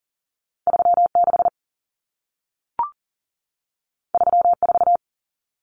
Morse code, heard as a keyed tone, about 40 wpm, 700 Hz. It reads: 36 34
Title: Numbers - 2-Digit 40wpm